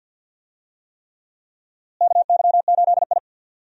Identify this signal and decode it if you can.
Morse code, heard as a keyed tone, about 46 wpm, 700 Hz. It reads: KY8I
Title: USA Call Signs - 4 Characters - Repeated 3x - 50wpm